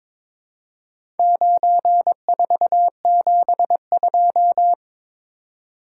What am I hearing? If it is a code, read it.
Morse code, heard as a keyed tone, about 22 wpm, 700 Hz. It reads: 9472
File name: practice / 22wpm / Four Letter-Number Combinations - 22wpm